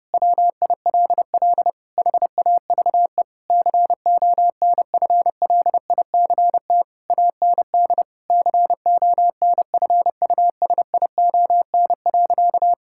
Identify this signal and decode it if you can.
Morse code, heard as a keyed tone, about 32 wpm, 700 Hz. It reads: WILL HAVE CONFLICT AND CONFUSION.
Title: Sentences from Top 1600 Words 30wpm